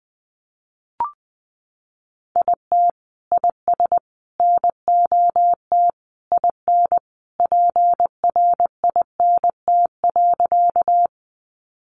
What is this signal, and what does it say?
Morse code, heard as a keyed tone, about 20 wpm, 700 Hz. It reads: IT IS NOT IN PRINT.
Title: Sentences from Top 1400 Words 20wpm